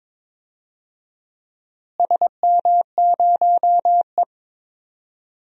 Morse code, heard as a keyed tone, about 21 wpm, 700 Hz. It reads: SM0E
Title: DXCC-100 Call Signs - 4 Characters - Repeated 3x - 22wpm